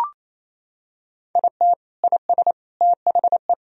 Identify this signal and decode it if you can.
Morse code, heard as a keyed tone, about 27 wpm, 700 Hz. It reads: IT IS THE
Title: Sentences from Top 1000 Words - Review Part 2 - 28wpm